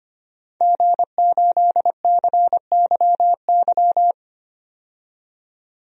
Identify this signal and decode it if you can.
Morse code, heard as a keyed tone, about 25 wpm, 700 Hz. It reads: G8CYY
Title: DXCC-100 Call Signs - 5 Characters - 25wpm